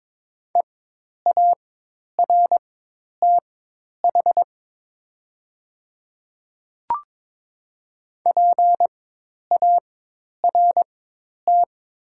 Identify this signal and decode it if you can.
Morse code, heard as a keyed tone, about 22 wpm, spaced as for 10 wpm, 700 Hz. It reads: EARTH PART
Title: 4x Character Spacing - Top 200 Words - 22wpm